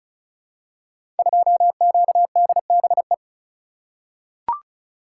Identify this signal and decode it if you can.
Morse code, heard as a keyed tone, about 36 wpm, 700 Hz. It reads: 2QDBE
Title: Code Groups - 5 Letter:Numbers - 35wpm